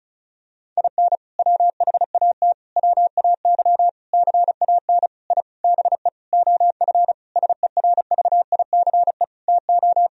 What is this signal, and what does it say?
Morse code, heard as a keyed tone, about 36 wpm, 700 Hz. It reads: IN WHAT WAY CAN I BE OF SERVICE TO
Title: Sentences from Top 1000 Words - Review Part 4 - 35wpm